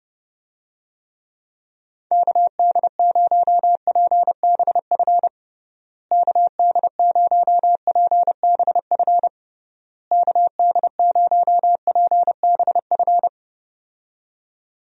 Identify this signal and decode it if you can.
Morse code, heard as a keyed tone, about 30 wpm, 700 Hz. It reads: KD0PBF KD0PBF KD0PBF
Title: USA Call Signs - 6 Characters - Repeated 3x - 30wpm